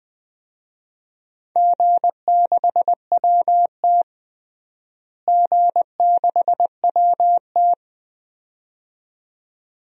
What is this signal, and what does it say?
Morse code, heard as a keyed tone, about 20 wpm, 700 Hz. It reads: G6WT G6WT
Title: DXCC-100 Call Signs - 4 Characters - Repeated 2x - 20wpm